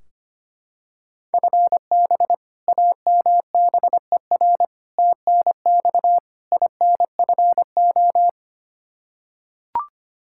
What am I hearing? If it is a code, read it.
Morse code, heard as a keyed tone, about 25 wpm, 700 Hz. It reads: FB AMBER TNX INFO
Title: Practice QSO - Exchange 3 - More Info - 25wpm